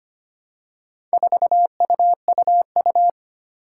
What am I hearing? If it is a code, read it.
Morse code, heard as a keyed tone, about 25 wpm, 700 Hz. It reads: 4UUU
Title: Four Challenging Letter-Number Combinations 25wpm